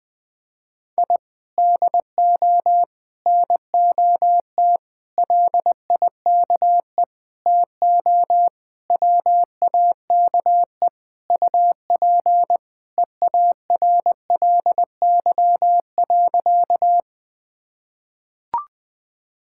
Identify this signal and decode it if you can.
Morse code, heard as a keyed tone, about 20 wpm, 700 Hz. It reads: I DO NOT LIKE TO WAKE UP EARLY.
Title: Sentences with 5-Word Phrases - Spoken Beginning - 20wpm